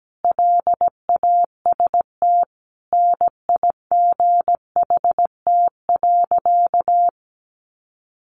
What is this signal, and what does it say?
Morse code, heard as a keyed tone, about 17 wpm, 700 Hz. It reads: LAST NIGHT.
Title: Sentences from Top 300 Words 17wpm